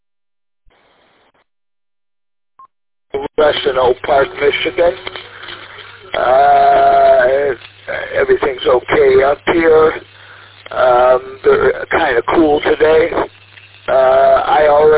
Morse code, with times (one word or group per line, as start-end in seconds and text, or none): none